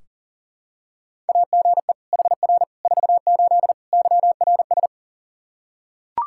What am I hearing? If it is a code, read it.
Morse code, heard as a keyed tone, about 40 wpm, 700 Hz. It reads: AGE HR 48 YRS
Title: Practice QSO - Exchange 3 - More Info - 40wpm